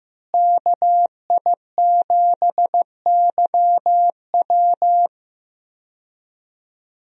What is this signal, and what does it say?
Morse code, heard as a keyed tone, about 15 wpm, 700 Hz. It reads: KI7YW
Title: USA Call Signs - 5 Characters - Repeated 3x - 15wpm